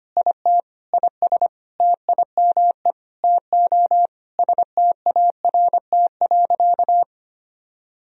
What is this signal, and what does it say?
Morse code, heard as a keyed tone, about 25 wpm, 700 Hz. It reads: IT IS TIME TO START.
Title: Sentences from Top 300 Words and Abbreviations 25wpm